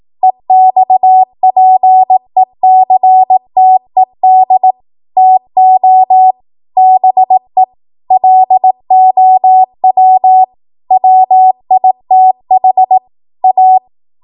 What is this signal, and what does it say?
Morse code, heard as a keyed tone, about 18 wpm, 750 Hz. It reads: EXPECTED TO BE LOW WITH A
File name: Morse / 18WPM